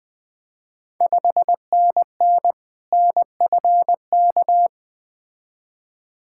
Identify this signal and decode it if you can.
Morse code, heard as a keyed tone, about 20 wpm, 700 Hz. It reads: 5NN NFK
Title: Ontario QSO Party Exchange 20wpm